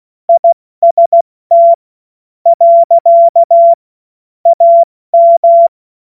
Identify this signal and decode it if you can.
Morse code, heard as a keyed tone, about 16 wpm, spaced as for 12 wpm, 680 Hz. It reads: IST. AM
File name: Morsekurs / Witze / Witz 27 Mai 2025 70 F80